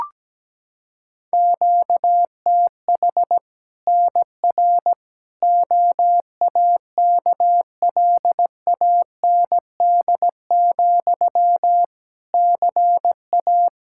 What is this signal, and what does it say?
Morse code, heard as a keyed tone, about 17 wpm, 700 Hz. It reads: QTH NR OAKLAND, CA
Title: K1USN SST QSO Practice 17wpm